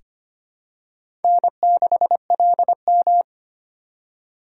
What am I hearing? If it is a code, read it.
Morse code, heard as a keyed tone, about 25 wpm, 700 Hz. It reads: N6LM
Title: USA Call Signs - 4 Characters - 2x Character Spacing - 25wpm